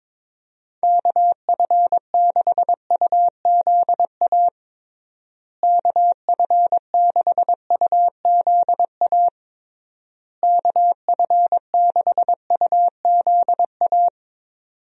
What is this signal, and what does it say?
Morse code, heard as a keyed tone, about 22 wpm, 700 Hz. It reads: KF6UZA KF6UZA KF6UZA
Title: USA Call Signs - 6 Characters - Repeated 3x - 22wpm